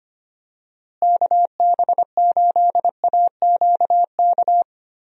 Morse code, heard as a keyed tone, about 25 wpm, 700 Hz. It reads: KB8AQK
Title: USA Call Signs - 6 Characters - Repeated 3x - 25wpm